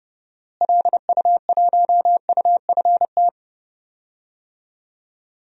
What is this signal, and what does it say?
Morse code, heard as a keyed tone, about 30 wpm, 700 Hz. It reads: LU1UFT
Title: DXCC-100 Call Signs - 6 Characters - Repeated 2x - 30wpm